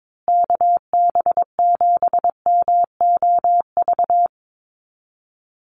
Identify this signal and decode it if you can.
Morse code, heard as a keyed tone, about 22 wpm, 700 Hz. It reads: KB7MOV
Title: USA Call Signs Repeated 3x 22wpm